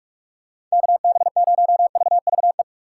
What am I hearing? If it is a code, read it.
Morse code, heard as a keyed tone, about 42 wpm, 700 Hz. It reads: KB0VVE
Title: USA Call Signs - 6 Characters - 6x Character Spacing - 45wpm